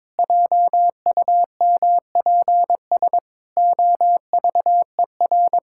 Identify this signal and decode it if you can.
Morse code, heard as a keyed tone, about 22 wpm, 700 Hz. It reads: JUMPS OVER
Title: Pangram Challenge v2 - 22wpm